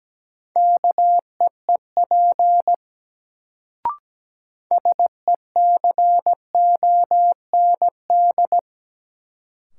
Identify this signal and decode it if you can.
Morse code, heard as a keyed tone, about 17 wpm, 700 Hz. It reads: KEEP SECOND
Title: Sets of 2 Words - New Word Drill - 17wpm